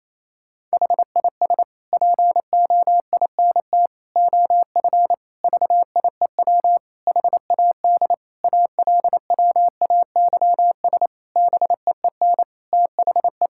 Morse code, heard as a keyed tone, about 28 wpm, 700 Hz. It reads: HIS POINT OF VIEW HAD ALWAYS BEEN THE
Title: Sentences from Top 1000 Words - Review Part 6 - 28wpm